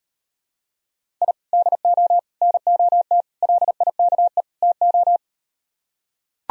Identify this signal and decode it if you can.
Morse code, heard as a keyed tone, about 38 wpm, 700 Hz. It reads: I DO NOT LIKE TO